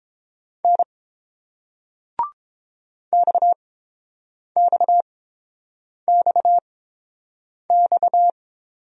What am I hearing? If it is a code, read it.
Morse code, heard as a keyed tone, about 31 wpm, spaced as for 7 wpm, 700 Hz. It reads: N XXXX